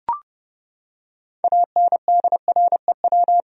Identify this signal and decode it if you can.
Morse code, heard as a keyed tone, about 30 wpm, 700 Hz. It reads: ANDREW
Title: USA Common Names - 30wpm